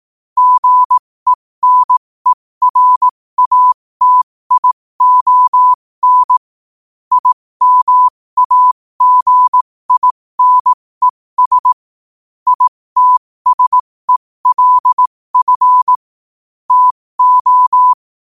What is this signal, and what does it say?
Morse code, heard as a keyed tone, about 18 wpm, spaced as for 15 wpm, 1000 Hz. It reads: GENERATION IMAGINES ITSELF TO
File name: 15wpm